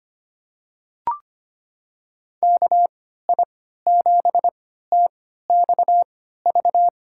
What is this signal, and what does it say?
Morse code, heard as a keyed tone, about 25 wpm, spaced as for 14 wpm, 700 Hz. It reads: KI7TXV
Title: USA Call Signs - 6 Characters - 3x Character Spacing - 25wpm